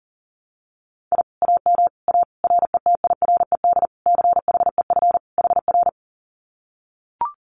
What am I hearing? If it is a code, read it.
Morse code, heard as a keyed tone, about 40 wpm, 700 Hz. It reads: I AM A RETIRED CHEF HR
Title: Common QSO Phrases Encore 40wpm